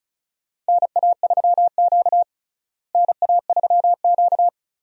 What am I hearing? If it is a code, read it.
Morse code, heard as a keyed tone, about 35 wpm, 700 Hz. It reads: NA3Q NA3Q
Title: USA Call Signs - 4 Characters - Repeated 2x - 35wpm